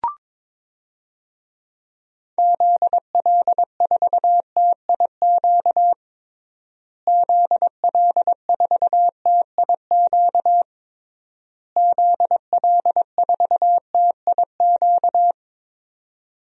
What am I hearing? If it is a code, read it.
Morse code, heard as a keyed tone, about 21 wpm, 700 Hz. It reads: ZL4TIQ ZL4TIQ ZL4TIQ
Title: DXCC-100 Call Signs - 6 Characters - Repeated 3x - 22wpm